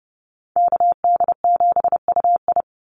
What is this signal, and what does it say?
Morse code, heard as a keyed tone, about 30 wpm, 700 Hz. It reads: KD7UI
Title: Call Signs - 30wpm